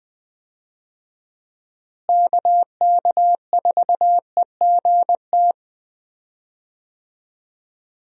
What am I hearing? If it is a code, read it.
Morse code, heard as a keyed tone, about 20 wpm, 700 Hz. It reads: KK4EGT
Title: USA Call Signs - 6 Characters - Repeated 3x - 20wpm